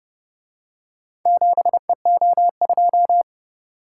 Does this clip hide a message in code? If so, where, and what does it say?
Yes: Morse code from 1.3 to 3.2 s, heard as a keyed tone, about 29 wpm, 700 Hz. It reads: 7EO2